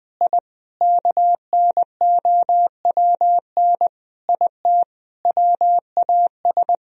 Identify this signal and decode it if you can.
Morse code, heard as a keyed tone, about 21 wpm, 700 Hz. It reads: I KNOWN IT WAS
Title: Sentences from Top 1000 Words - Review Part 3 - 20wpm